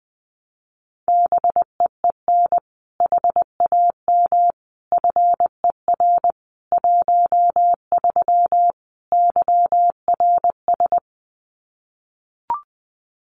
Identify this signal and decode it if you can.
Morse code, heard as a keyed tone, about 20 wpm, 700 Hz. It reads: BEEN HAM FER 13 YRS
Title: Common QSO Phrases 20wpm